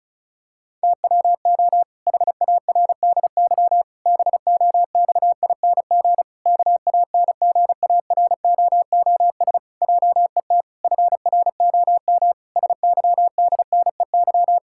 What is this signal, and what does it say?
Morse code, heard as a keyed tone, about 34 wpm, 700 Hz. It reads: TWO HARDY BOXING KANGAROOS JET FROM SYDNEY
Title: Pangram Challenge v2 - 35wpm